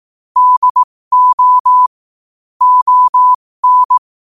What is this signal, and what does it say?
Morse code, heard as a keyed tone, about 18 wpm, spaced as for 15 wpm, 1000 Hz. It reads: DO ON
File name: qotdfiles / 15wpm